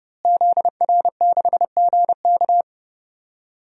Morse code, heard as a keyed tone, about 28 wpm, 700 Hz. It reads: ZR6GK
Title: DXCC-100 Call Signs - 5 Characters - Repeated 3x - 30wpm